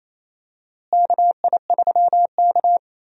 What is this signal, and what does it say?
Morse code, heard as a keyed tone, about 28 wpm, 700 Hz. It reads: KI3K
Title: USA Call Signs - 4 Characters - Repeated 3x - 28wpm